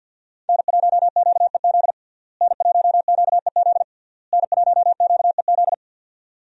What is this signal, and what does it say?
Morse code, heard as a keyed tone, about 46 wpm, 700 Hz. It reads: N1QEZ N1QEZ N1QEZ
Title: USA Call Signs - 5 Characters - Repeated 3x - 50wpm